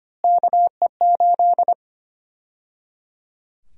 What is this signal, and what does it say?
Morse code, heard as a keyed tone, about 25 wpm, 700 Hz. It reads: KE8
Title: USA Call Sign Prefixes - Repeated 2x - 25wpm